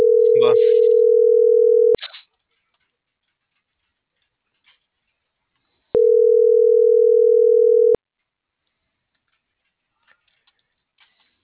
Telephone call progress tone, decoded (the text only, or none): ringback tone